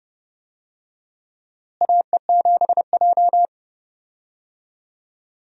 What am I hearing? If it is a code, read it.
Morse code, heard as a keyed tone, about 31 wpm, 700 Hz. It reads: AE7J
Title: USA Call Signs - 4 Characters - Repeated 2x - 30wpm